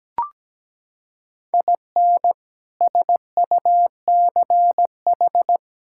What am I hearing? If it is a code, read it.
Morse code, heard as a keyed tone, about 17 wpm, 700 Hz. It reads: IN SUCH
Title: Sentences from Top 1000 Words - Review Part 5 - 17wpm